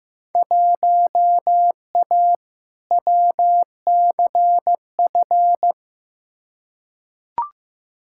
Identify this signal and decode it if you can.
Morse code, heard as a keyed tone, about 15 wpm, 700 Hz. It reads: 1A WCF
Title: Field Day Exchanges - 15wpm